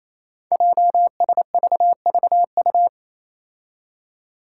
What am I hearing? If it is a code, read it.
Morse code, heard as a keyed tone, about 27 wpm, 700 Hz. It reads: JSVVU
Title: Five Challenging Letter-Number Combinations 28wpm